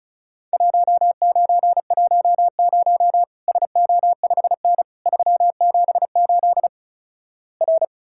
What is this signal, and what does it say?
Morse code, heard as a keyed tone, about 36 wpm, 700 Hz. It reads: 1910 SO5N 378 R
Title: Worked All Europe DX Contest - QTC Exchange - 35wpm